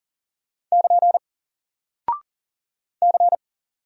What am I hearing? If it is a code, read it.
Morse code, heard as a keyed tone, about 39 wpm, 700 Hz. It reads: <KN> C
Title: Single Character (Letter:Number:Punctuation:Prosign) 40wpm